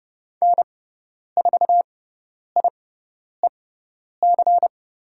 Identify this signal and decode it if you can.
Morse code, heard as a keyed tone, about 30 wpm, spaced as for 10 wpm, 700 Hz. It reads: N4IEC
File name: Lesson 17g - Letter u - Call Signs 30wpm